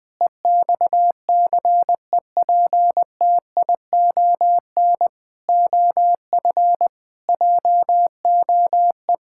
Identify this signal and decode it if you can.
Morse code, heard as a keyed tone, about 20 wpm, 700 Hz. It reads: EXCEPTION OF JOE
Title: Sentences with 4-Word Phrases - 20wpm